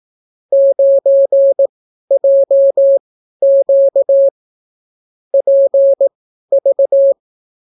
Morse code, heard as a keyed tone, about 18 wpm, spaced as for 12 wpm, 550 Hz. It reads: 9JQ PV